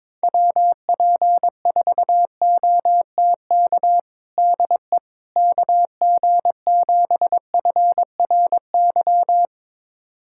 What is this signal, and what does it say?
Morse code, heard as a keyed tone, about 21 wpm, 700 Hz. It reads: WP4OTK DE KG7FRY
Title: USA Callsign DE Callsign - 6 Characters - 22wpm